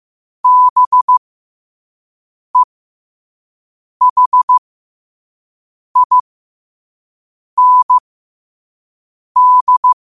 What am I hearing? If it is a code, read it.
Morse code, heard as a keyed tone, about 15 wpm, spaced as for 5 wpm, 1000 Hz. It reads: BEHIND
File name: qotdfiles / 5wpm